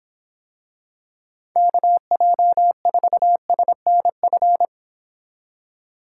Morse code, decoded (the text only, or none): KJ4SNF